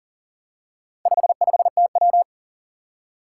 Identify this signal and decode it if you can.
Morse code, heard as a keyed tone, about 43 wpm, 700 Hz. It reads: 55TW
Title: Four Challenging Letter-Number Combinations 40wpm